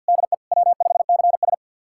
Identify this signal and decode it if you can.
Morse code, heard as a keyed tone, about 46 wpm, 700 Hz. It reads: DE W5CS
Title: USA Callsign DE Callsign - 4 Characters - 50wpm